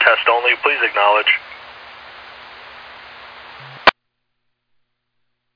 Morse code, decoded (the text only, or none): none